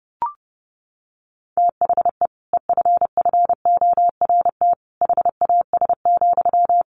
Morse code, heard as a keyed tone, about 30 wpm, 700 Hz. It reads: THE EFFORT HAS,
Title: Sentences from Top 1200 Words 30wpm